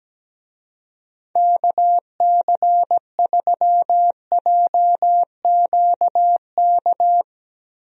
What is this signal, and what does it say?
Morse code, heard as a keyed tone, about 17 wpm, 700 Hz. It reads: KC3JQK
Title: USA Call Signs - 6 Characters - Repeated 3x - 17wpm